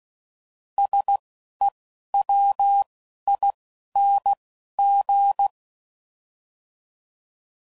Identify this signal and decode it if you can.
Morse code, heard as a keyed tone, about 16 wpm, 800 Hz. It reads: S E W I N G
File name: Suffixes / Special INGs / Single INGs